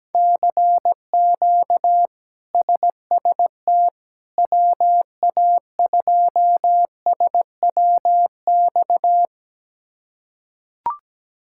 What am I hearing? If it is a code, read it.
Morse code, heard as a keyed tone, about 17 wpm, 700 Hz. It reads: CQ SST WA2SWX
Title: K1USN SST QSO Practice - 17wpm